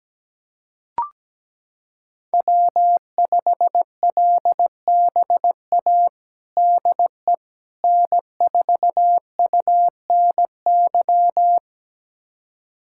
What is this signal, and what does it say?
Morse code, heard as a keyed tone, about 17 wpm, 700 Hz. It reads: W5LBA DE N4UNY
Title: USA Callsign DE Callsign - 5 Characters - 17wpm